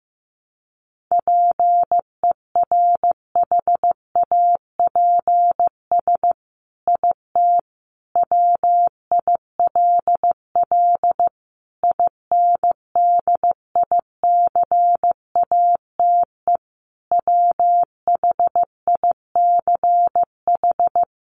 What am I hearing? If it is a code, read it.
Morse code, heard as a keyed tone, about 15 wpm, 700 Hz. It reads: PERHAPS IT WILL INDICATE WHICH
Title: Sentences from Top 1300 Words 15wpm